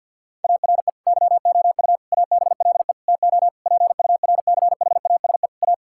Morse code, heard as a keyed tone, about 52 wpm, 700 Hz. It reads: ARE YOU ABLE TO PURCHASE A
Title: Sentences from Top 1000 Words - Review Part 5 - 50wpm